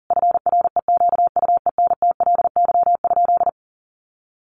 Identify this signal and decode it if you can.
Morse code, heard as a keyed tone, about 40 wpm, 700 Hz. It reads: FREQUENTLY?
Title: Sentences from Top 1300 Words 40wpm